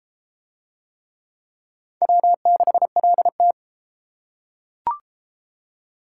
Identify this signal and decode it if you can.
Morse code, heard as a keyed tone, about 32 wpm, 700 Hz. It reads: W6LT